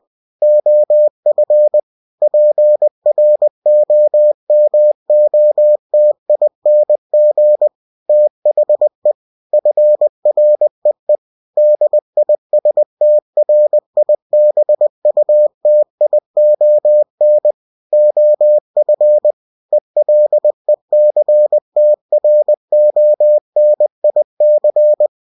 Morse code, heard as a keyed tone, about 20 wpm, 600 Hz. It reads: OF PROMOTING THE FREE DISTRIBUTION OF ELECTRONIC
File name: cw-books / winnie-the-pooh / winnie-20wpm-page-0113